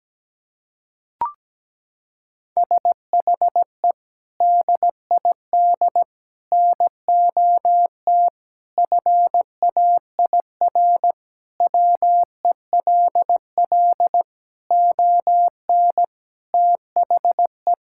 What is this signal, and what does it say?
Morse code, heard as a keyed tone, about 17 wpm, 700 Hz. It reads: SHE DID NOT FAIR WELL ON THE
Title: Sentences from Top 1000 Words - Review Part 5 - 17wpm